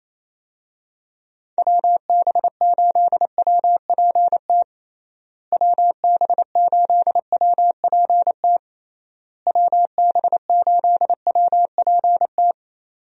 Morse code, heard as a keyed tone, about 28 wpm, 700 Hz. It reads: WB8WPT WB8WPT WB8WPT
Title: USA Call Signs - 6 Characters - Repeated 3x - 28wpm